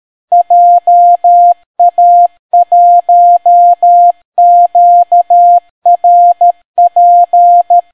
Morse code, heard as a keyed tone, about 13 wpm, 700 Hz. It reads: JA1QRP